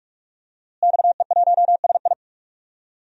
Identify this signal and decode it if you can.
Morse code, heard as a keyed tone, about 46 wpm, 700 Hz. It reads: XE1SI
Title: DXCC-100 Call Signs - 5 Characters - Repeated 3x - 45wpm